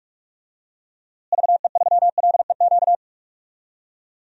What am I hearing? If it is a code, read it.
Morse code, heard as a keyed tone, about 44 wpm, 700 Hz. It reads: VE3LEQ
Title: DXCC-100 Call Signs - 6 Characters - Repeated 3x - 45wpm